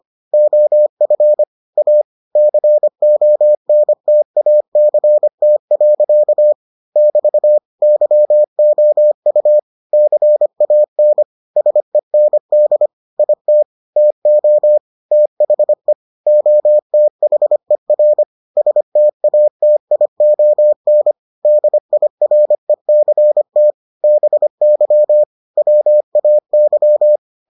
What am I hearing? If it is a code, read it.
Morse code, heard as a keyed tone, about 25 wpm, 600 Hz. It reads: OF A CONTACT. <BT> YOU CAN SEND IT TO THE OTHER STATION DIRECT BY WAY